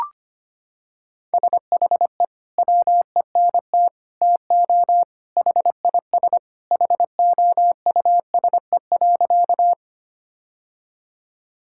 Morse code, heard as a keyed tone, about 25 wpm, 700 Hz. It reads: SHE WENT TO HIS HOUSE.
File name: practice / 25wpm / Sentences with 2-Word Phrases - 25wpm